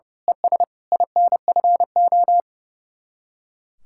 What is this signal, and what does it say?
Morse code, heard as a keyed tone, about 34 wpm, 700 Hz. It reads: ES INFO
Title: Speed-Racing Common QSO Phrases - 30wpm